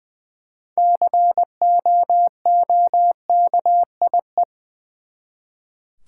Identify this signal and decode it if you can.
Morse code, heard as a keyed tone, about 20 wpm, 700 Hz. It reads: COOKIE
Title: Six Letter Words - 20wpm